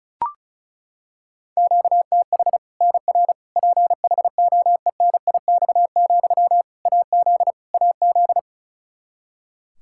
Morse code, heard as a keyed tone, about 36 wpm, 700 Hz. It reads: QTH NR PHOENIX, AZ AZ
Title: K1USN SST QSO Practice 35wpm